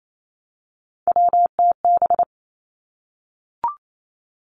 Morse code, heard as a keyed tone, about 28 wpm, 700 Hz. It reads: WTB
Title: Three Challenging Letter-Number Combinations 28wpm